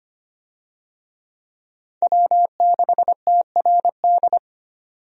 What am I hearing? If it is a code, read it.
Morse code, heard as a keyed tone, about 25 wpm, 700 Hz. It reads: W6TRD
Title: USA Call Signs - 5 Characters - 25wpm